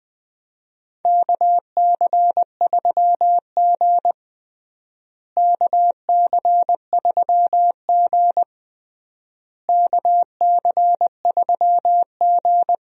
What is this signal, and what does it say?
Morse code, heard as a keyed tone, about 20 wpm, 700 Hz. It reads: KC3G KC3G KC3G
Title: USA Call Signs - 4 Characters - Repeated 3x - 20wpm